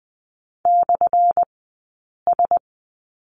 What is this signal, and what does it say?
Morse code, heard as a keyed tone, about 20 wpm, spaced as for 8 wpm, 700 Hz. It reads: /S
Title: 183 Lesson 30d - Letter b - Two Characters 20wpm